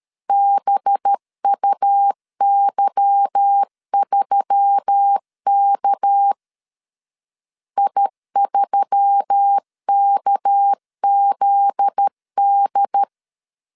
Morse code, heard as a keyed tone, about 13 wpm, 800 Hz. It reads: BUY3K I3KZD
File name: LEZIONI / 8 YJ45 / 8 Y 12wpm 2228